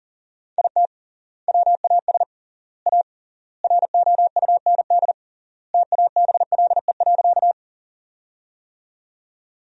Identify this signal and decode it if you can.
Morse code, heard as a keyed tone, about 40 wpm, 700 Hz. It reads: IT WAS A ROUND TABLE.
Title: 3x Word-Spacing - Sentences from Top 400 Words - 40wpm